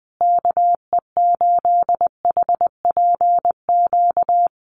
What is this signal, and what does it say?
Morse code, heard as a keyed tone, about 20 wpm, 700 Hz. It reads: KE8HPQ
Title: USA Call Signs Repeated 3x 20wpm